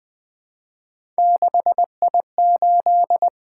Morse code, heard as a keyed tone, about 20 wpm, 700 Hz. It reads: 6I8
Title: Three Letter-Number Combinations - 20wpm